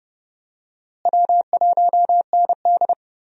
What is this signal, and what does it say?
Morse code, heard as a keyed tone, about 30 wpm, 700 Hz. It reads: W1ND
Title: USA Call Signs - 4 Characters - Repeated 2x - 30wpm